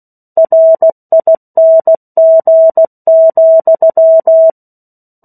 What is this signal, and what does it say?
Morse code, heard as a keyed tone, about 16 wpm, 650 Hz. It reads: RING,
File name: 16 WPM